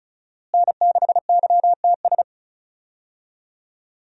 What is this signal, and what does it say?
Morse code, heard as a keyed tone, about 36 wpm, 700 Hz. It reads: N6YTS
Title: USA Call Signs - 5 Characters - Repeated 3x - 35wpm